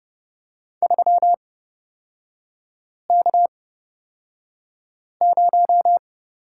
Morse code, heard as a keyed tone, about 30 wpm, spaced as for 5 wpm, 700 Hz. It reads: 3K0
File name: Lesson 37e - Letter x - Three Characters 30wpm